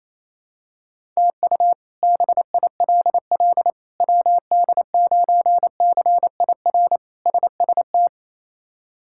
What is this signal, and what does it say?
Morse code, heard as a keyed tone, about 28 wpm, 700 Hz. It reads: TU BILL WD9CIR SST
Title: K1USN SST QSO Practice - 28wpm